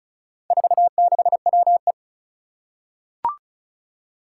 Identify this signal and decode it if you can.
Morse code, heard as a keyed tone, about 36 wpm, 700 Hz. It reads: 46WE
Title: Four Challenging Letter-Number Combinations 35wpm